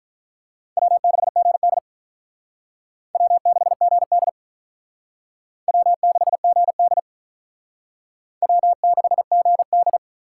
Morse code, heard as a keyed tone, about 45 wpm, 700 Hz. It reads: W6GD W6GD W6GD W6GD